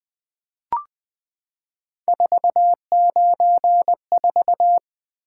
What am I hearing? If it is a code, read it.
Morse code, heard as a keyed tone, about 21 wpm, 700 Hz. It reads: 494
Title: Three Letter-Number Combinations - 20wpm